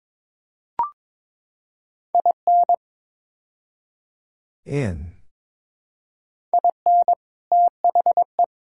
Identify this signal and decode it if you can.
Morse code, heard as a keyed tone, about 22 wpm, 700 Hz. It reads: IN IN THE
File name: practice / 22wpm / Build A Phrase - 5-Word Phrases - 22wpm